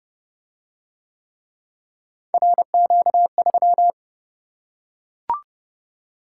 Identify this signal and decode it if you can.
Morse code, heard as a keyed tone, about 26 wpm, 700 Hz. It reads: RQ3